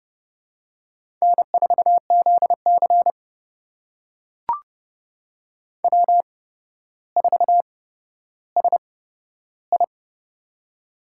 Morse code, heard as a keyed tone, about 30 wpm, spaced as for 13 wpm, 700 Hz. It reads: N4ZC W4SI
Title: USA Call Signs - 4 Characters - 8x Character Spacing - 30wpm